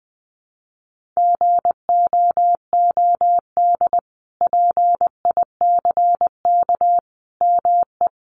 Morse code, heard as a keyed tone, about 20 wpm, 700 Hz. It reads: GOOD PICK ME
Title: Sentences from Top 1000 Words 20wpm